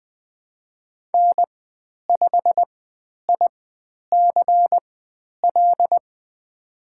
Morse code, heard as a keyed tone, about 20 wpm, spaced as for 10 wpm, 700 Hz. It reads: N5ICL
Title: 095 Lesson 16g - Letter c - Call Signs 20wpm